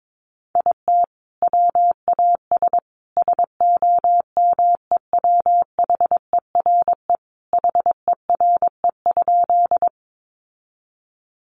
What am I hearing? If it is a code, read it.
Morse code, heard as a keyed tone, about 22 wpm, 700 Hz. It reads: IT WAS SOMEWHERE HERE?
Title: Sentences from Top 1400 Words 22wpm